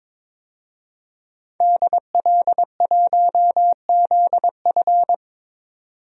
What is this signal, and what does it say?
Morse code, heard as a keyed tone, about 22 wpm, 700 Hz. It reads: DL1ZF
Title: DXCC-100 Call Signs - 5 Characters - 22wpm